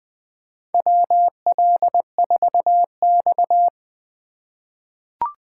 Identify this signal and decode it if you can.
Morse code, heard as a keyed tone, about 20 wpm, 700 Hz. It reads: WL4X
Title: USA Call Signs - 4 Characters - 3x Character Spacing - 20wpm